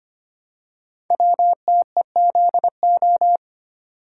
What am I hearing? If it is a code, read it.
Morse code, heard as a keyed tone, about 25 wpm, 700 Hz. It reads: WTEZO
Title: Code Groups - 5 Letters - 25wpm